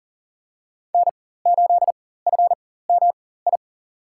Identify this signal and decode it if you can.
Morse code, heard as a keyed tone, about 37 wpm, spaced as for 18 wpm, 700 Hz. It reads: N8FMI
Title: USA Call Signs - 5 Characters - 4x Character Spacing - 40wpm